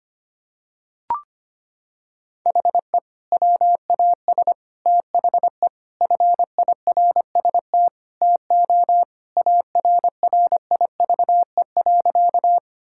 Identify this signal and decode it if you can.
Morse code, heard as a keyed tone, about 25 wpm, 700 Hz. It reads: HE WAS THE FIRST TO ARRIVE.
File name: practice / 25wpm / Sentences with 2-Word Phrases - 25wpm